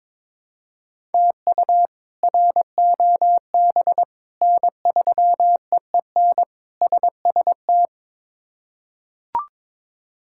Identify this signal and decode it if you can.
Morse code, heard as a keyed tone, about 22 wpm, 700 Hz. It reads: TU ROB N3EEN SST
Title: K1USN SST QSO Practice - 22wpm